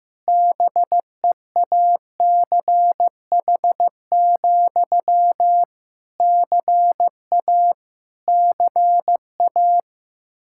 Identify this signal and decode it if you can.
Morse code, heard as a keyed tone, about 15 wpm, 700 Hz. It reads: BEACH, CA CA
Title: Practice QSO - Exchange 1 - Beginning - 15wpm